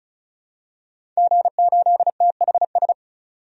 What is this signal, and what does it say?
Morse code, heard as a keyed tone, about 36 wpm, 700 Hz. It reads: G8THS
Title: DXCC-100 Call Signs - 5 Characters - 35wpm